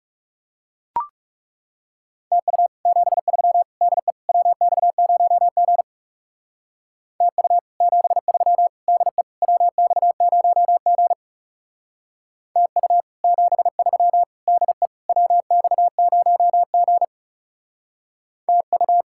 Morse code, heard as a keyed tone, about 35 wpm, 700 Hz. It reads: TU 73 DE WX0G TU 73 DE WX0G TU 73 DE WX0G TU
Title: Speed-Racing Common QSO Phrases Encore - 30wpm